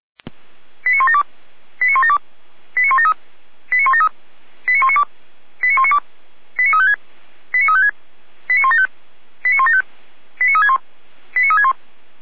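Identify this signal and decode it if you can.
Five-tone selective call (ZVEI), heard as quick runs of five tones, tones of about 70 ms. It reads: 89172 89172 89173 89173 89192 89192 89356 89356 89175 89175 89361 89361
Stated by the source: ZVEI five-tone selective call received by SDR.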